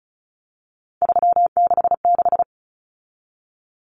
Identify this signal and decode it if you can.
Morse code, heard as a keyed tone, about 34 wpm, 700 Hz. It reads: 366
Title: Numbers - 3-Digit 35wpm